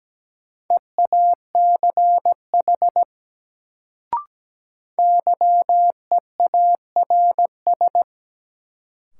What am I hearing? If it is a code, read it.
Morse code, heard as a keyed tone, about 17 wpm, 700 Hz. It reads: EACH YEARS
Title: Sentences from Top 200 Words - New Word Drill - 17wpm